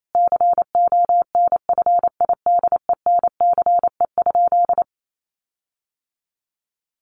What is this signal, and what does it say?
Morse code, heard as a keyed tone, about 28 wpm, 700 Hz. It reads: CONFIDENCE?
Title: Sentences from Top 1500 Words 28wpm